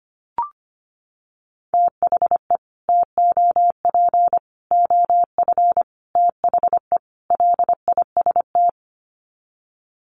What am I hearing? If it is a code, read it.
Morse code, heard as a keyed tone, about 25 wpm, 700 Hz. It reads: THE TOP OF THE LIST
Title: Sets of 5 Words Encore2 25wpm